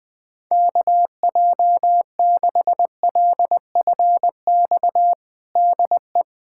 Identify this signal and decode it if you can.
Morse code, heard as a keyed tone, about 20 wpm, 700 Hz. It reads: KJ6LFX DE
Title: USA Callsign DE Callsign - 6 Characters - 20wpm